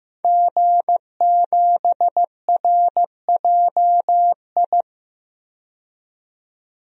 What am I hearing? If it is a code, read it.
Morse code, heard as a keyed tone, about 15 wpm, 700 Hz. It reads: G7RJI
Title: DXCC-100 Call Signs - 5 Characters - 15wpm